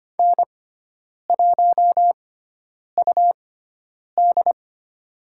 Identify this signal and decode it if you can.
Morse code, heard as a keyed tone, about 25 wpm, spaced as for 9 wpm, 700 Hz. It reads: N1UD
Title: USA Call Signs - 4 Characters - 6x Character Spacing - 25wpm